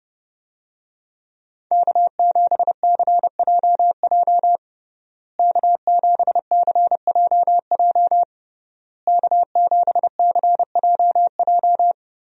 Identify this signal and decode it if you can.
Morse code, heard as a keyed tone, about 30 wpm, 700 Hz. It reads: K7CJJ K7CJJ K7CJJ
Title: USA Call Signs - 5 Characters - Repeated 3x - 30wpm